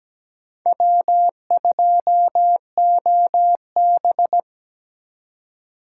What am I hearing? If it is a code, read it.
Morse code, heard as a keyed tone, about 17 wpm, 700 Hz. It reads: W2OB
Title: USA Call Signs - 4 Characters - 4x Character Spacing - 17wpm